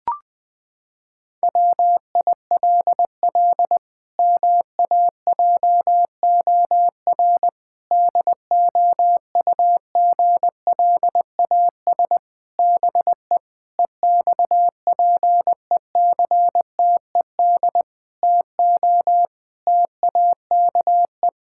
Morse code, heard as a keyed tone, about 20 wpm, 700 Hz. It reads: WILL MAJOR DOUGLAS BE EXPECTED TO TAKE
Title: Pangram Challenge v2 - 20wpm